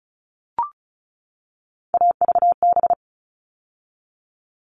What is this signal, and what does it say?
Morse code, heard as a keyed tone, about 34 wpm, 700 Hz. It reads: AVB
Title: All Cut Numbers - 3-Digit 35wpm